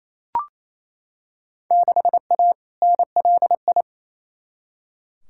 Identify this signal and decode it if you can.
Morse code, heard as a keyed tone, about 28 wpm, 700 Hz. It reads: 6A NLI
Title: Field Day Exchanges Encore - 28wpm